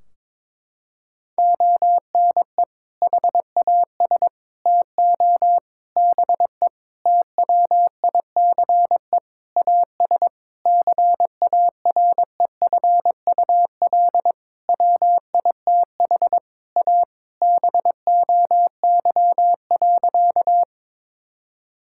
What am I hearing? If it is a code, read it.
Morse code, heard as a keyed tone, about 22 wpm, 700 Hz. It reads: ONE HAS TO BE TWICE AS CAREFUL WITH A BOY.
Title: Sentences from Top 1600 Words 22wpm